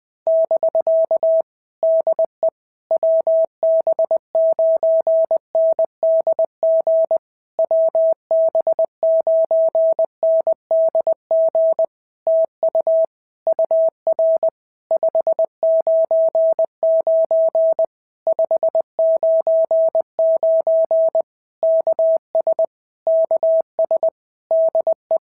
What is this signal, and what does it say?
Morse code, heard as a keyed tone, about 20 wpm, 650 Hz. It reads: <BK> DE WB9NDG WB9NDG TU UR 599 599 KS KS DE
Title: POTA QSO Practice - 20wpm